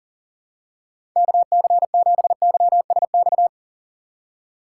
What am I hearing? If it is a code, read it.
Morse code, heard as a keyed tone, about 40 wpm, 700 Hz. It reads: KC7YSX
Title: USA Call Signs - 6 Characters - 3x Character Spacing - 40wpm